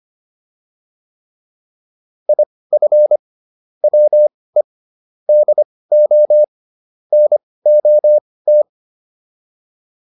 Morse code, heard as a keyed tone, about 25 wpm, spaced as for 18 wpm, 600 Hz. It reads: IF WE DO NOT